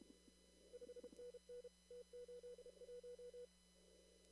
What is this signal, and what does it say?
Morse code, heard as a keyed tone, about 32 wpm, 490 Hz. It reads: 5NN T81
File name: CW / kt5j wpx cw-op-k5pi